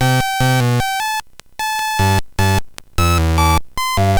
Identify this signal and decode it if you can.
There is no code.